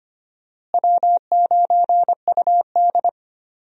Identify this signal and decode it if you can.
Morse code, heard as a keyed tone, about 25 wpm, 700 Hz. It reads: W9UD
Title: USA Call Signs - 4 Characters - Repeated 3x - 25wpm